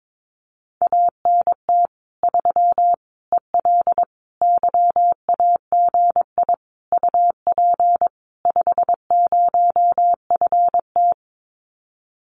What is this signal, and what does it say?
Morse code, heard as a keyed tone, about 22 wpm, 700 Hz. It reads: ANT 3 EL YAGI UP 50FT
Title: Common QSO Phrases Encore 22wpm